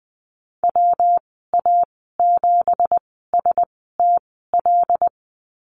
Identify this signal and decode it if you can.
Morse code, heard as a keyed tone, about 20 wpm, spaced as for 15 wpm, 700 Hz. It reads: WA7STL
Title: 179 Lesson 29g - Slash - Call Signs 20wpm